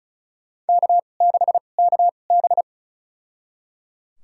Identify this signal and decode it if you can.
Morse code, heard as a keyed tone, about 35 wpm, spaced as for 25 wpm, 700 Hz. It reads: K6KB
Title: USA Call Signs - 4 Characters - 2x Character Spacing - 35wpm